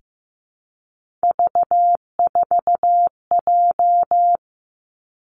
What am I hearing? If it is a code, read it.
Morse code, heard as a keyed tone, about 15 wpm, 700 Hz. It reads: V4J
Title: Three Challenging Letter-Number Combinations 15wpm